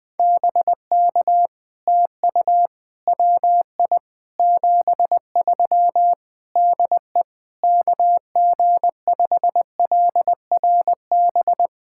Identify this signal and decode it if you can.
Morse code, heard as a keyed tone, about 20 wpm, 700 Hz. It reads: BK TU WI 73 DE KG5LRB
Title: POTA QSO Practice - 20wpm